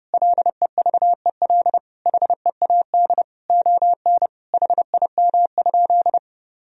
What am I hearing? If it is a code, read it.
Morse code, heard as a keyed tone, about 30 wpm, 700 Hz. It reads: LEVEL HEAD ON HIM?
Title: Sentences from Top 1000 Words - Review Part 4 - 30wpm